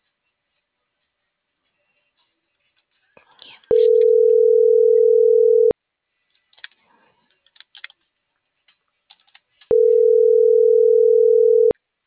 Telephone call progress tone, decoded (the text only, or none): ringback tone